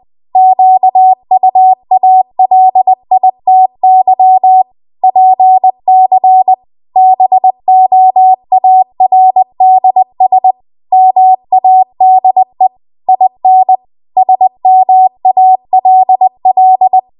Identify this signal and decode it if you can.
Morse code, heard as a keyed tone, about 20 wpm, 750 Hz. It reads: QUALITY PC BOARDS MADE IN SMALL
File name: Archive / 20 WPM / 20WPM